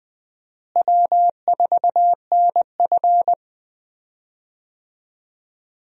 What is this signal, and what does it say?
Morse code, heard as a keyed tone, about 20 wpm, 700 Hz. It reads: W4NF
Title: USA Call Signs - 4 Characters - Repeated 2x - 20wpm